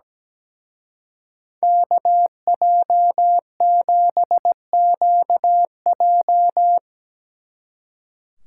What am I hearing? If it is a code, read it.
Morse code, heard as a keyed tone, about 19 wpm, 700 Hz. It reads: KJ7QJ
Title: Speed Racing - USA Call Signs 17wpm